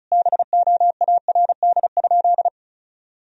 Morse code, heard as a keyed tone, about 36 wpm, 700 Hz. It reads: BOARD?
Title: Sentences from Top 1000 Words - Review Part 4 - 35wpm